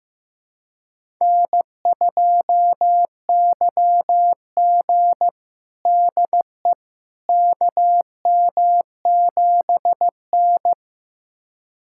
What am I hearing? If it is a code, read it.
Morse code, heard as a keyed tone, about 15 wpm, 700 Hz. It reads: N2YG DE KM7N
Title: USA Callsign DE Callsign - 4 Characters - 15wpm